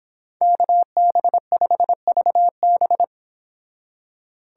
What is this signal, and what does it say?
Morse code, heard as a keyed tone, about 31 wpm, 700 Hz. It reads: KB5VB